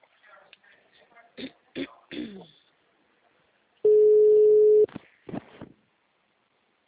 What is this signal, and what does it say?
Ringback tone (Europe, 425 Hz).
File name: out- -2019- - - .10944